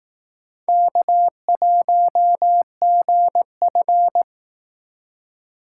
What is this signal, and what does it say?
Morse code, heard as a keyed tone, about 19 wpm, 700 Hz. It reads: K1GF